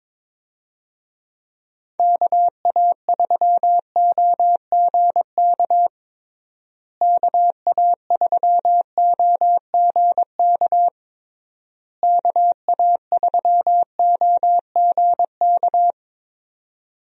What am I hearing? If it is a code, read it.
Morse code, heard as a keyed tone, about 21 wpm, 700 Hz. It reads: KA3OGK KA3OGK KA3OGK
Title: USA Call Signs - 6 Characters - Repeated 3x - 22wpm